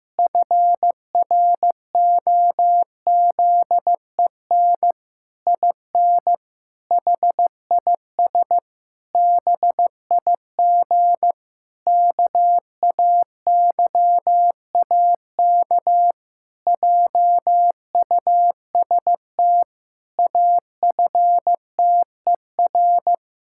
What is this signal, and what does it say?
Morse code, heard as a keyed tone, about 15 wpm, 700 Hz. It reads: FROZEN IN HIS BIG KAYAK JUST AFTER
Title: Pangram Challenge v2 - 15wpm